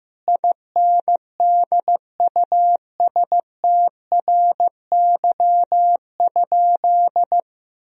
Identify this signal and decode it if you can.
Morse code, heard as a keyed tone, about 16 wpm, 700 Hz. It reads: INDUSTRY?
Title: Sentences from Top 1000 Words - Review Part 4 - 15wpm